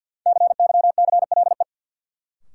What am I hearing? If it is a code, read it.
Morse code, heard as a keyed tone, about 48 wpm, 700 Hz. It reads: CYCLE
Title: Sentences from Top 900 Words - New Word Drill - 50wpm